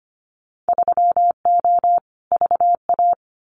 Morse code, heard as a keyed tone, about 25 wpm, 700 Hz. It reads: 3O VA
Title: Winter Field Day Exchanges 25wpm